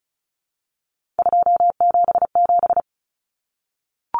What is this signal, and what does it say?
Morse code, heard as a keyed tone, about 34 wpm, 700 Hz. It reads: 277
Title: Common Cut Numbers - 3-Digit 35wpm